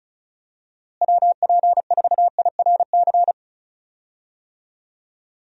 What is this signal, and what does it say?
Morse code, heard as a keyed tone, about 36 wpm, 700 Hz. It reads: WP4IRC
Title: USA Call Signs - 6 Characters - Repeated 3x - 35wpm